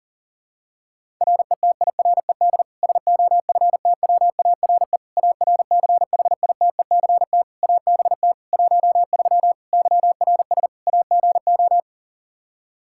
Morse code, heard as a keyed tone, about 40 wpm, 700 Hz. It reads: RETIRED SOFTWARE ARCHITECT ABT 13 YRS AGO
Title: Practice QSO - Exchange 3 - More Info - 40wpm